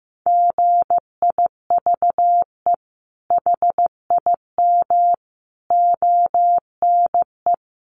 Morse code, heard as a keyed tone, about 15 wpm, 700 Hz. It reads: GIVE HIM ONE
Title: Sentences from Top 1200 Words 15wpm